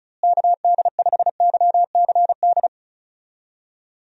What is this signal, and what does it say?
Morse code, heard as a keyed tone, about 36 wpm, 700 Hz. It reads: KD5YCD
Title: USA Call Signs - 6 Characters - Repeated 2x - 35wpm